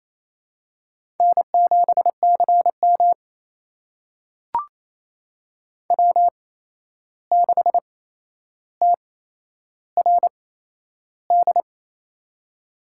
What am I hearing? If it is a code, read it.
Morse code, heard as a keyed tone, about 28 wpm, spaced as for 11 wpm, 700 Hz. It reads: N7CM W6TRD
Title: USA Call Signs - 8x Character Spacing - 28wpm